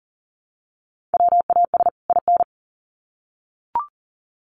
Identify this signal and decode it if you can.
Morse code, heard as a keyed tone, about 40 wpm, 700 Hz. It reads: WAS IN
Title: Sets of 2 Words 40wpm